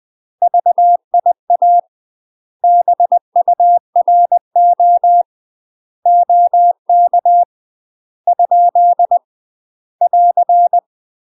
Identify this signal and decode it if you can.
Morse code, heard as a keyed tone, about 20 wpm, 700 Hz. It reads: VIA BURO OK ? <AR>
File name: UK MASTER QSOs TRAINING / 20WPM / UK MASTER QSOs 0046